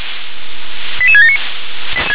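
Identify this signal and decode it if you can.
Five-tone selective call (ZVEI), heard as quick runs of five tones, tones of about 80 ms. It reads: 88679
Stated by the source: ZVEI five-tone selective call 88671 received on SDR.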